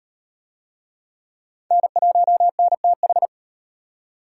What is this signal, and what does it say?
Morse code, heard as a keyed tone, about 38 wpm, 700 Hz. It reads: N1NTH